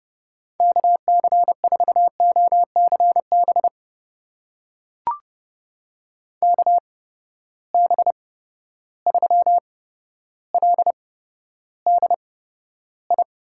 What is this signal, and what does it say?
Morse code, heard as a keyed tone, about 29 wpm, spaced as for 13 wpm, 700 Hz. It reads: KC4OCB KB3LDI
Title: USA Call Signs - 6 Characters - 8x Character Spacing - 30wpm